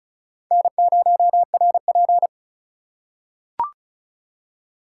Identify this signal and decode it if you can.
Morse code, heard as a keyed tone, about 36 wpm, 700 Hz. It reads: N0RP
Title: USA Call Signs - 2x Character Spacing - 35wpm